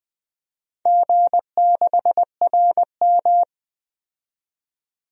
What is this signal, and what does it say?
Morse code, heard as a keyed tone, about 20 wpm, 700 Hz. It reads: G6RM
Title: DXCC-100 Call Signs - 4 Characters - 20wpm